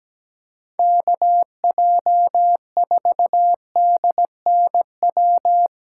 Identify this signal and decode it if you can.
Morse code, heard as a keyed tone, about 17 wpm, 700 Hz. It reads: KJ4DNW
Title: USA Callsign DE Callsign - 6 Characters - 17wpm